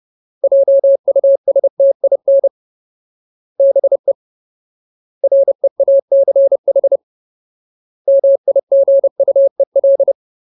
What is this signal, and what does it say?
Morse code, heard as a keyed tone, about 30 wpm, 550 Hz. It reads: JUSTIN BE REACH MIGUEL
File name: Ultimate Word List / 30WPM Ultimate